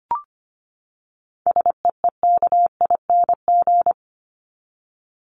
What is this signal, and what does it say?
Morse code, heard as a keyed tone, about 25 wpm, 700 Hz. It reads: SEEKING
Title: ING Suffix - Top 100 Words with Base Words 25wpm